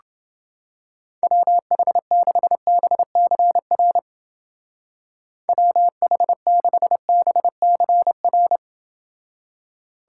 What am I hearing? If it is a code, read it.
Morse code, heard as a keyed tone, about 29 wpm, 700 Hz. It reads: WH6BCR WH6BCR